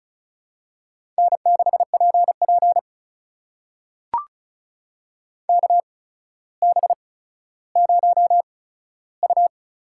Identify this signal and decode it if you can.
Morse code, heard as a keyed tone, about 36 wpm, spaced as for 15 wpm, 700 Hz. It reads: N6PP KB0U
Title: USA Call Signs - 8x Character Spacing - 35wpm